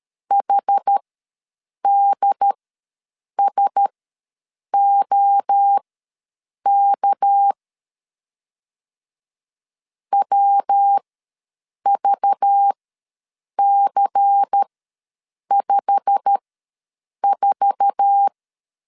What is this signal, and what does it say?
Morse code, heard as a keyed tone, about 13 wpm, spaced as for 7 wpm, 800 Hz. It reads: HDSOK WVC54
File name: LEZIONI / 8 YJ45 / 8 YJ45 ws 12wpm 3146